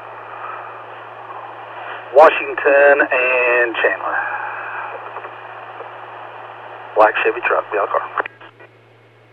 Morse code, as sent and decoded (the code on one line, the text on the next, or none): none